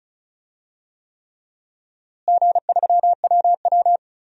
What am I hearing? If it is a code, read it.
Morse code, heard as a keyed tone, about 36 wpm, 700 Hz. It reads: G3WW
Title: DXCC-100 Call Signs - 4 Characters - Repeated 3x - 35wpm